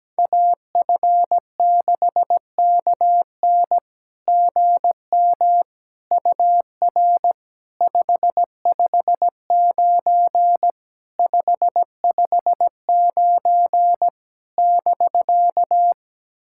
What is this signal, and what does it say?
Morse code, heard as a keyed tone, about 17 wpm, 700 Hz. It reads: AF6KN GM UR 559 559 <BK>
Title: SOTA QSO Practice 17wpm